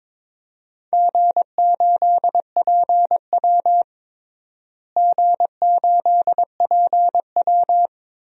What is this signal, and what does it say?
Morse code, heard as a keyed tone, about 22 wpm, 700 Hz. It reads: G8PW G8PW
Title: DXCC-100 Call Signs - 4 Characters - Repeated 2x - 22wpm